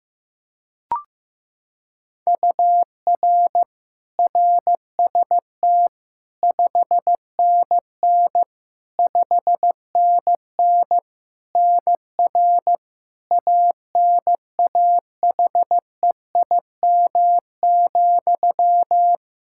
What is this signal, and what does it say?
Morse code, heard as a keyed tone, about 15 wpm, 700 Hz. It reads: UR RST 5NN 5NN NR ANAHEIM,
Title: Practice QSO - Exchange 1 - Beginning - 15wpm